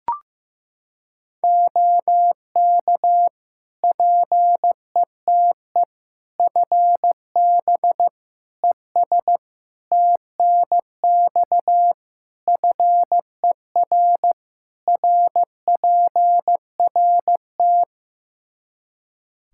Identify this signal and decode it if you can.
Morse code, heard as a keyed tone, about 15 wpm, 700 Hz. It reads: OK PETE FB ES TNX FER RPRT
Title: Practice QSO - Exchange 2 - Info - 15wpm